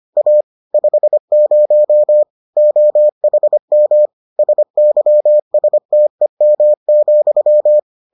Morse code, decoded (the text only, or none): A 50 OHM SYSTEM,